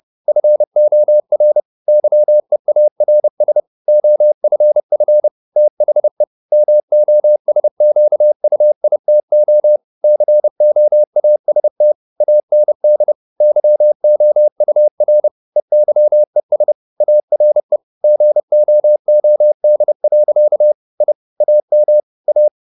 Morse code, heard as a keyed tone, about 30 wpm, 600 Hz. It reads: FOR YEARS OFF THE MOSQUITO COAST AND YOUR EYES ARE GOOD. I AM A
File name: the old man and the sea / the old man and the sea 1 30wpm